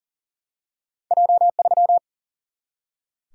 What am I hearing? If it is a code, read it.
Morse code, heard as a keyed tone, about 40 wpm, 700 Hz. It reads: J3
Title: Single-Letter Single-Number 40wpm